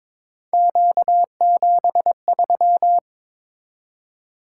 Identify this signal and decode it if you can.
Morse code, heard as a keyed tone, about 22 wpm, 700 Hz. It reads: Q73
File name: practice / 22wpm / Three Letter-Number Combinations - 22wpm